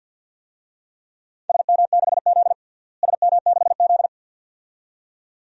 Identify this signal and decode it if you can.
Morse code, heard as a keyed tone, about 52 wpm, 700 Hz. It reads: SM6Z SM6Z
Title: DXCC-100 Call Signs - 4 Characters - Repeated 2x - 50wpm